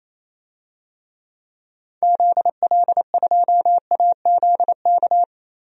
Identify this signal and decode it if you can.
Morse code, heard as a keyed tone, about 28 wpm, 700 Hz. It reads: ZL2AZK
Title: DXCC-100 Call Signs - 6 Characters - 28wpm